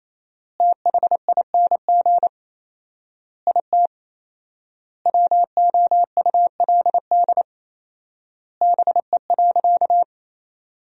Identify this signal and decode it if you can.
Morse code, heard as a keyed tone, about 28 wpm, 700 Hz. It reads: THING IT WOULD BE.
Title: 4x Word-Spacing - Sentences from Top 200 Words - 28wpm